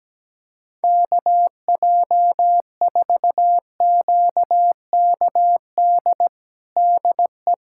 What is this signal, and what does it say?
Morse code, heard as a keyed tone, about 17 wpm, 700 Hz. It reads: KJ4QKD DE
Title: USA Callsign DE Callsign - 6 Characters - 17wpm